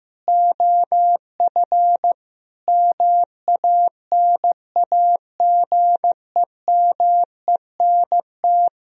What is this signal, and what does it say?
Morse code, heard as a keyed tone, about 15 wpm, 700 Hz. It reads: OF MANAGEMENT
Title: Sentences from Top 1000 Words - Review Part 4 - 15wpm